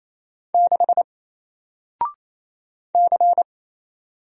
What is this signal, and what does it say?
Morse code, heard as a keyed tone, about 28 wpm, 700 Hz. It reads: 6 C
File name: Single-Letter-Number-28wpm